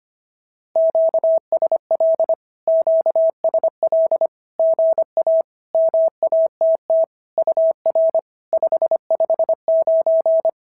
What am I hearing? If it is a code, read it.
Morse code, heard as a keyed tone, about 25 wpm, 650 Hz. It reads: QSL QSL GA MATT UR 559
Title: SOTA QSO Practice 25wpm